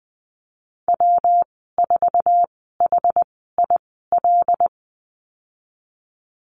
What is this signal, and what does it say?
Morse code, heard as a keyed tone, about 20 wpm, spaced as for 15 wpm, 700 Hz. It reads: W4HIL
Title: 147 Lesson 24g - Letter y - Call Signs 20wpm